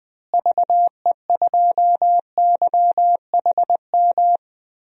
Morse code, heard as a keyed tone, about 20 wpm, 700 Hz. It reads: VE2YHM
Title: DXCC-100 Call Signs - 6 Characters - Repeated 2x - 20wpm